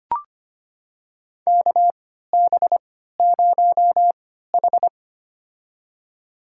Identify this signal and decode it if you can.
Morse code, heard as a keyed tone, about 25 wpm, spaced as for 14 wpm, 700 Hz. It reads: KB0H
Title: USA Call Signs - 3x Character Spacing - 25wpm